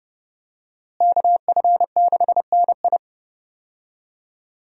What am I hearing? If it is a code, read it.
Morse code, heard as a keyed tone, about 30 wpm, 700 Hz. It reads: KF6NI
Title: USA Call Signs - 5 Characters - Repeated 3x - 30wpm